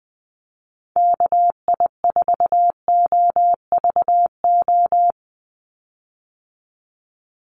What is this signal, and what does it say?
Morse code, heard as a keyed tone, about 20 wpm, 700 Hz. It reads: KI4OVO
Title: USA Call Signs 20wpm